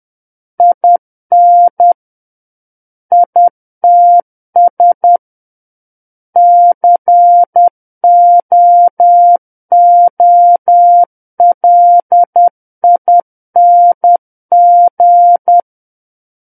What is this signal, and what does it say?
Morse code, heard as a keyed tone, about 10 wpm, 700 Hz. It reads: IN ITS COOLING
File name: Book One / 01-Chapter 1 - 10 WPM